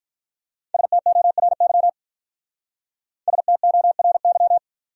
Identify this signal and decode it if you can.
Morse code, heard as a keyed tone, about 43 wpm, 700 Hz. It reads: STORY STORY
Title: Speed-Racing Top 300 Words 35wpm